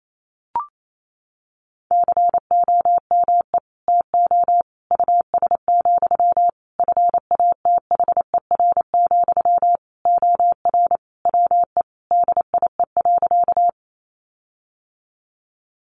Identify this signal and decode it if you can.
Morse code, heard as a keyed tone, about 28 wpm, 700 Hz. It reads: COME TO US, FATHER, OR WE DIE.
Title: Sentences from Top 1300 Words 28wpm